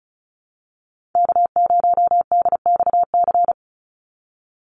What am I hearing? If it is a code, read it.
Morse code, heard as a keyed tone, about 34 wpm, 700 Hz. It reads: K0DXC
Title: Call Signs Repeated 2x - 35wpm